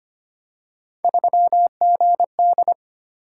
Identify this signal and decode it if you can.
Morse code, heard as a keyed tone, about 25 wpm, 700 Hz. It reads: 3GD
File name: Three Letter-Number Combinations - 25wpm